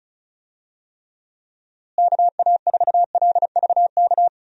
Morse code, heard as a keyed tone, about 36 wpm, 700 Hz. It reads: KA4LVK
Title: USA Call Signs - 6 Characters - Repeated 3x - 35wpm